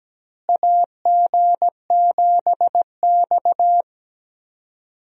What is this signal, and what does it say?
Morse code, heard as a keyed tone, about 17 wpm, 700 Hz. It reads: AG7X
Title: USA Call Signs - 4 Characters - Repeated 3x - 17wpm